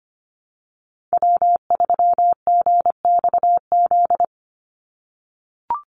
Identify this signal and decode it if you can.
Morse code, heard as a keyed tone, about 25 wpm, 700 Hz. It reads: W3GXZ
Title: USA Call Signs 25wpm